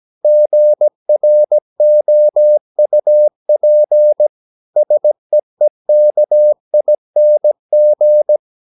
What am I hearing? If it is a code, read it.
Morse code, heard as a keyed tone, about 17 wpm, 600 Hz. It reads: GROUP SEEKING
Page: Romanen Never af Ken Follet på engelsk.